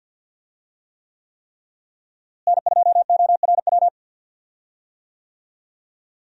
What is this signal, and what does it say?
Morse code, heard as a keyed tone, about 46 wpm, 700 Hz. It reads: N2ORW
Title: USA Call Signs - 5 Characters - Repeated 3x - 50wpm